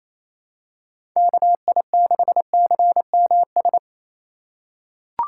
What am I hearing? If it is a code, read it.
Morse code, heard as a keyed tone, about 27 wpm, 700 Hz. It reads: KI6CMS
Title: USA Call Signs - 3x Character Spacing - 28wpm